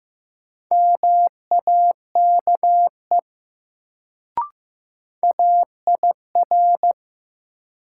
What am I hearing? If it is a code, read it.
Morse code, heard as a keyed tone, about 15 wpm, 700 Hz. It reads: MAKE AIR
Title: Sets of 2 Words - New Word Drill - 15wpm